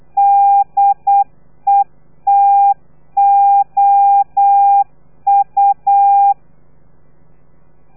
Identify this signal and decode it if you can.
Morse code, heard as a keyed tone, about 8 wpm, 800 Hz. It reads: DETOU
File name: telegrafia / i2tfj / Lez3 35